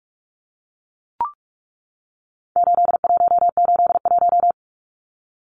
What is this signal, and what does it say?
Morse code, heard as a keyed tone, about 44 wpm, 700 Hz. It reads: 8181
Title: Numbers - 4-Digit 45wpm